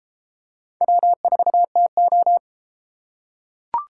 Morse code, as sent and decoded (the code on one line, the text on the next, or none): .-- ....- - ---
W4TO